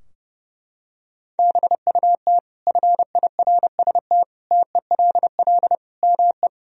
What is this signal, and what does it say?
Morse code, heard as a keyed tone, about 32 wpm, 700 Hz. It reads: BUT FIRST TELL ME
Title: Sentences from Top 1000 Words - Review Part 2 - 30wpm